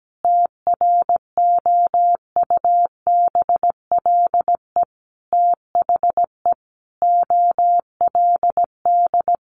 Morse code, heard as a keyed tone, about 17 wpm, 700 Hz. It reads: TROUBLE THE OLD
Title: Sentences from Top 1200 Words 17wpm